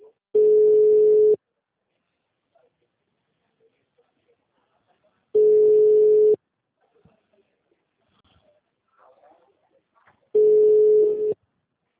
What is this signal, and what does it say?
Ringback tone (Europe, 425 Hz).